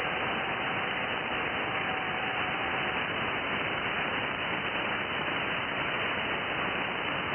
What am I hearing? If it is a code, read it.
FT8.
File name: IC-7100